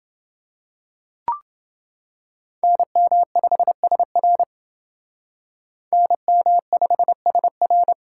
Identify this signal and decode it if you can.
Morse code, heard as a keyed tone, about 28 wpm, 700 Hz. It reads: NM5SR NM5SR